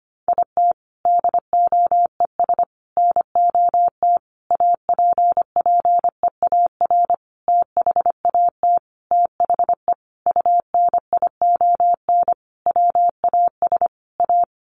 Morse code, heard as a keyed tone, about 25 wpm, 700 Hz. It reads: IT DOES NOT APPEAR THAT THE UNION WAS A
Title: Sentences from Top 1400 Words 25wpm